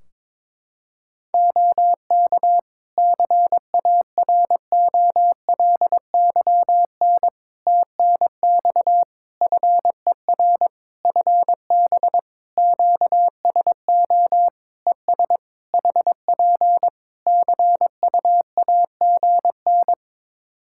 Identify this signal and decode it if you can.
Morse code, heard as a keyed tone, about 22 wpm, 700 Hz. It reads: OK CAROLYN TNX FER FB QSO ES HP CUAGN
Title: Practice QSO - Exchange 4 - Ending - 22wpm